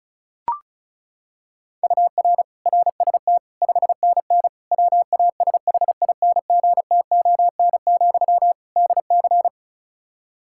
Morse code, heard as a keyed tone, about 36 wpm, 700 Hz. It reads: UR RST 5NN WASHINGTON, DC
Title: Practice QSO - Exchange 1 - Beginning - 35wpm